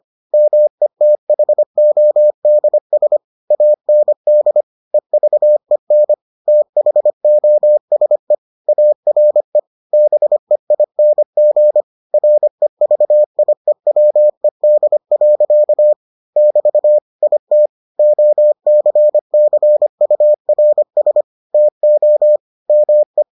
Morse code, heard as a keyed tone, about 25 wpm, 600 Hz. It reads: METHODS AND EVEN THOSE ARE BEING REVIEWED. <BT> IT OCCURS TO ME